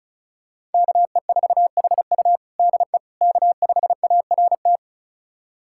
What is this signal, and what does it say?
Morse code, heard as a keyed tone, about 36 wpm, 700 Hz. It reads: KE4HU DE K5ART
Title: USA Callsign DE Callsign - 5 Characters - 35wpm